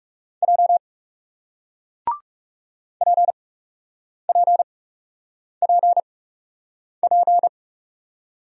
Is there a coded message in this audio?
Yes: Morse code from 0.4 to 7.5 s, heard as a keyed tone, about 40 wpm, spaced as for 8 wpm, 700 Hz. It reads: J PPPP